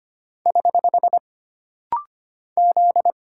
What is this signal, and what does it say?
Morse code, heard as a keyed tone, about 25 wpm, 700 Hz. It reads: <HH> Z
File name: Single Character (Letter:Number:Punctuation:Prosign) - Rapid-Fire - 25wpm